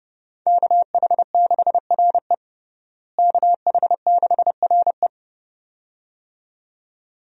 Morse code, heard as a keyed tone, about 30 wpm, 700 Hz. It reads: KH6RE KH6RE
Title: USA Call Signs - 5 Characters - Repeated 2x - 30wpm